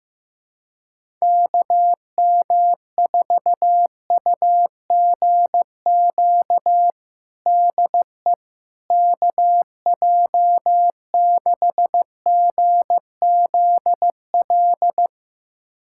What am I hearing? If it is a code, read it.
Morse code, heard as a keyed tone, about 16 wpm, 700 Hz. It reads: KM4UGQ DE KJ6GZL
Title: USA Callsign DE Callsign 15wpm